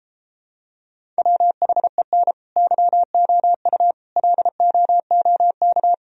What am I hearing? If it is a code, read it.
Morse code, heard as a keyed tone, about 34 wpm, 700 Hz. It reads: WHEN YOU LOOK